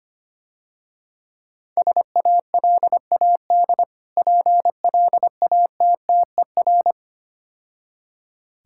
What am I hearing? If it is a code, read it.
Morse code, heard as a keyed tone, about 25 wpm, 700 Hz. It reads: SALAD PLATTER
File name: practice / 25wpm / Difficult Words - Speed-Racing with Hints - 25wpm